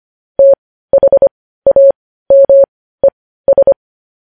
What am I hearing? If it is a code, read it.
Morse code, heard as a keyed tone, about 25 wpm, spaced as for 15 wpm, 550 Hz. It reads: THAMES
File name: podcast / News headlines-1x-25x15